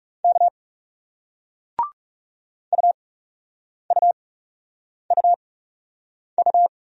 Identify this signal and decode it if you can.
Morse code, heard as a keyed tone, about 42 wpm, spaced as for 8 wpm, 700 Hz. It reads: K UUUU